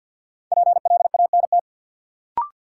Morse code, heard as a keyed tone, about 52 wpm, 700 Hz. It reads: PLANT
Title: Sentences from Top 1000 Words - Review Part 1 - New Word Drill - 50wpm